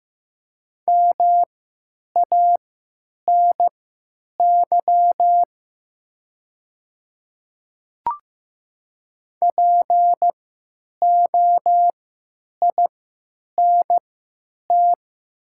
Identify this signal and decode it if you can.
Morse code, heard as a keyed tone, about 15 wpm, spaced as for 9 wpm, 700 Hz. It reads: MANY POINT
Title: 3x Character Spacing - Top 300 Words - 15wpm